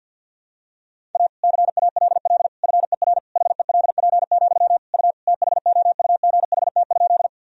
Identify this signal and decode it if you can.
Morse code, heard as a keyed tone, about 46 wpm, 700 Hz. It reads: A CALL FER HELP, U THOUGHT?
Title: Sentences from Top 300 Words and Abbreviations 50wpm